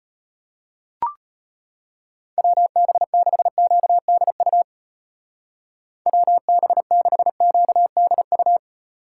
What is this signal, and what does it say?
Morse code, heard as a keyed tone, about 32 wpm, 700 Hz. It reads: WB6QDU WB6QDU